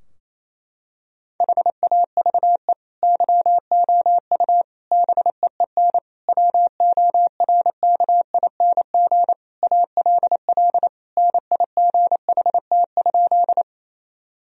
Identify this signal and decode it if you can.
Morse code, heard as a keyed tone, about 28 wpm, 700 Hz. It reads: HAVE YOU BEEN WORKING ALL NIGHT?
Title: Sentences from Top 1000 Words - Review Part 4 - 28wpm